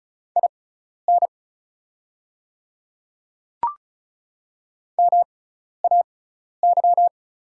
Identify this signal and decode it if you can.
Morse code, heard as a keyed tone, about 34 wpm, spaced as for 12 wpm, 700 Hz. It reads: IN MAY
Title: 6x Character Spacing - Top 100 Words - 35wpm